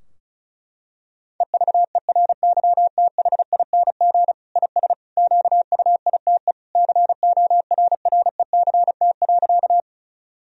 Morse code, heard as a keyed tone, about 36 wpm, 700 Hz. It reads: EVERYTHING IS QUITE CORRECT.
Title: Sentences from Top 1000 Words - Review Part 2 - 35wpm